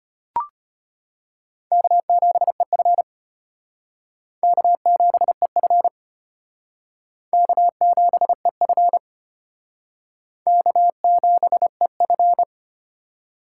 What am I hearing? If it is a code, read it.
Morse code, heard as a keyed tone, about 31 wpm, 700 Hz. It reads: K7EF K7EF K7EF K7EF